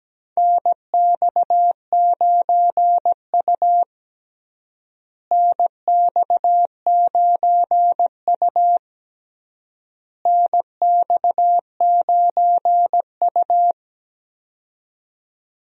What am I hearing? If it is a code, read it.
Morse code, heard as a keyed tone, about 17 wpm, 700 Hz. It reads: NX9U NX9U NX9U
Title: USA Call Signs - 4 Characters - Repeated 3x - 17wpm